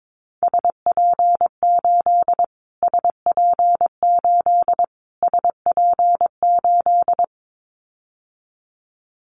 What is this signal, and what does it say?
Morse code, heard as a keyed tone, about 22 wpm, 700 Hz. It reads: SP8 SP8 SP8
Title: Callsign Prefixes for Top 25 DXCC - Repeated 3x - 22wpm